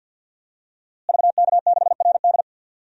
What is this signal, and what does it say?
Morse code, heard as a keyed tone, about 48 wpm, 700 Hz. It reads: VK6RD
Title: DXCC-100 Call Signs - 5 Characters - Repeated 3x - 50wpm